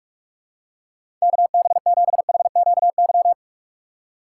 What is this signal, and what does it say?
Morse code, heard as a keyed tone, about 47 wpm, 700 Hz. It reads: KB7HQY
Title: USA Call Signs - 6 Characters - 6x Character Spacing - 45wpm